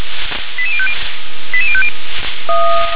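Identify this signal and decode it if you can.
Five-tone selective call (ZVEI), heard as quick runs of five tones, tones of about 80 ms. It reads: 88050 88050
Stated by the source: ZVEI five-tone selective call 88126, received on an SDR.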